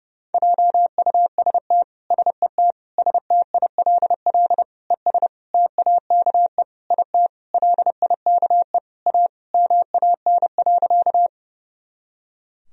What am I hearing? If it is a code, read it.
Morse code, heard as a keyed tone, about 30 wpm, 700 Hz. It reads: JUST SET STILL ES TAKE IT LIKE A MAN.
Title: Sentences from Top 300 Words and Abbreviations 30wpm